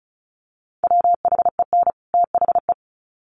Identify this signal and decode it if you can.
Morse code, heard as a keyed tone, about 34 wpm, 700 Hz. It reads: WHEN THE
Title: Sets of 2 Words 35wpm